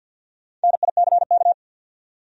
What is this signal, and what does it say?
Morse code, heard as a keyed tone, about 49 wpm, 700 Hz. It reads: NICK
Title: USA Common Names - Max 5 Letters - 50wpm